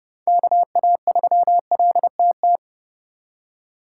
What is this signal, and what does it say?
Morse code, heard as a keyed tone, about 36 wpm, 700 Hz. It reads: KA3LTT
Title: Speed Racing - USA 2x3 Call Signs 30wpm